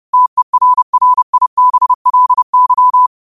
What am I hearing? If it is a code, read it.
Morse code, heard as a keyed tone, about 30 wpm, 1000 Hz. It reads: TERRIBLY